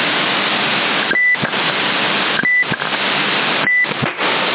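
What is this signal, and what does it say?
Emergency Alert System (SAME) tones, heard as harsh buzzing bursts. It is heard from 1.0 to 3.9 s.
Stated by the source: Received on 27075 kHz CB in Sassari via HDSDR software receiver.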